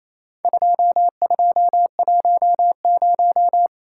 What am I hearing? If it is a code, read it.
Morse code, heard as a keyed tone, about 28 wpm, 700 Hz. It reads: 2210
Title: Worked All Europe DX Contest - QTC Exchange - 28wpm